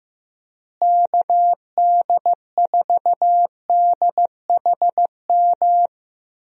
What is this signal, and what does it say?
Morse code, heard as a keyed tone, about 16 wpm, 700 Hz. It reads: KD4DHM
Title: USA Call Signs - 6 Characters - Repeated 2x - 15wpm